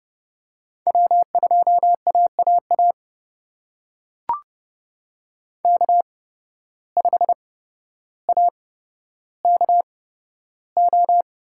Morse code, heard as a keyed tone, about 31 wpm, spaced as for 13 wpm, 700 Hz. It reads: W2AAA K5AKO
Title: USA Call Signs - 5 Characters - 8x Character Spacing - 30wpm